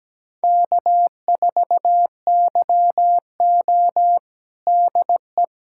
Morse code, heard as a keyed tone, about 17 wpm, 700 Hz. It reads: K4YO DE
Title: USA Callsign DE Callsign - 4 Characters - 17wpm